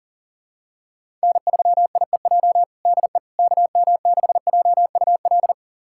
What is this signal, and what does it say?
Morse code, heard as a keyed tone, about 40 wpm, 700 Hz. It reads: N3IEJ DE KM6JUL
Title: USA Callsign DE Callsign 40wpm